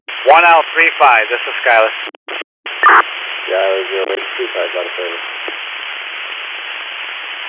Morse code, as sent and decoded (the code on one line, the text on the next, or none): none